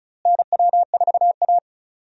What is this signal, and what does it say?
Morse code, heard as a keyed tone, about 36 wpm, 700 Hz. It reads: NW4A
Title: USA Call Signs - 4 Characters - Repeated 3x - 35wpm